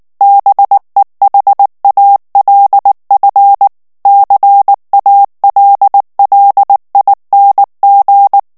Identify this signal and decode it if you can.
Morse code, heard as a keyed tone, about 19 wpm, 800 Hz. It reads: BEHALF CALLING
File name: ANGLAIS texte clair 3 vitesse 18 mots minute